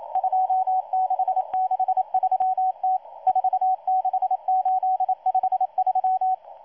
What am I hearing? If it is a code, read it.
Morse code, heard as a keyed tone, about 28 wpm, 750 Hz. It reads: 2663T 46853
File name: M12- -1310utc-13926khz-error-restart